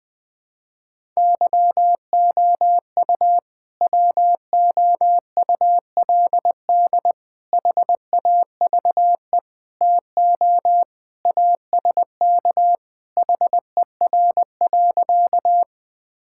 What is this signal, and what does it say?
Morse code, heard as a keyed tone, about 20 wpm, 700 Hz. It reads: YOU WOULD HAVE TO ASK HER.
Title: Sentences with 3-Word Phrases - Spoken Beginning - 20wpm